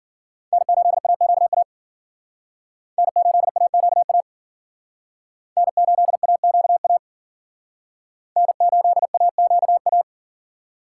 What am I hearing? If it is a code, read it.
Morse code, heard as a keyed tone, about 51 wpm, 700 Hz. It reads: N8AQA N8AQA N8AQA N8AQA